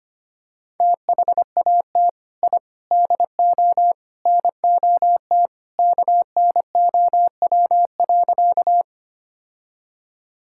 Morse code, heard as a keyed tone, about 25 wpm, 700 Hz. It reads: THAT I DO NOT KNOW.
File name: practice / 25wpm / Sentences with 4-Word Phrases - 25wpm